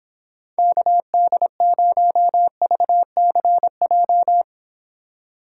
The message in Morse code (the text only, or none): KD0VCJ